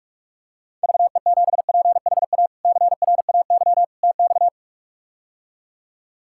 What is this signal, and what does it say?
Morse code, heard as a keyed tone, about 47 wpm, 700 Hz. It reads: VE7PHA CRAY TX
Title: K1USN SST QSO Practice - 45wpm